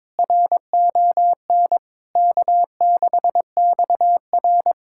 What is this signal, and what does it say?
Morse code, heard as a keyed tone, about 22 wpm, 700 Hz. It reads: RON K6XR
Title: K1USN SST QSO Practice - 22wpm